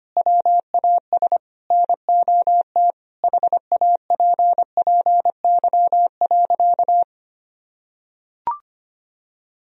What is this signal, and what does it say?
Morse code, heard as a keyed tone, about 25 wpm, 700 Hz. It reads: WAS NOT HAPPY.
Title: Sentences from Top 1000 Words - Review Part 3 - 25wpm